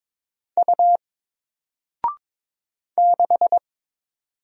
Morse code, heard as a keyed tone, about 22 wpm, 700 Hz. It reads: U 6
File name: Single-Letter-Number-22wpm